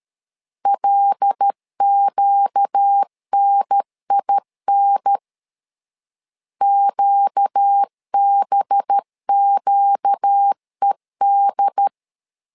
Morse code, heard as a keyed tone, about 13 wpm, 800 Hz. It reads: LQNIN QBQED
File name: LEZIONI / 5 LBVC / 5 LB 12wpm 1906